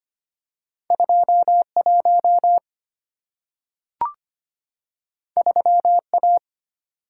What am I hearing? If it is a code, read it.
Morse code, heard as a keyed tone, about 25 wpm, 690 Hz. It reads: 21 3A
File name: Two Letter-Number Combinations - 25wpm